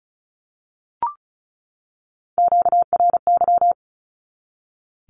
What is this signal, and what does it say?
Morse code, heard as a keyed tone, about 34 wpm, 700 Hz. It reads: QRY
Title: Q-Codes 35wpm